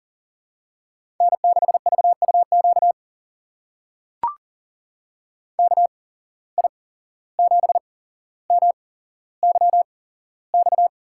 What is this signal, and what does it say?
Morse code, heard as a keyed tone, about 38 wpm, spaced as for 16 wpm, 700 Hz. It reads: N6VUQ KI7MYX
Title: USA Call Signs - 8x Character Spacing - 40wpm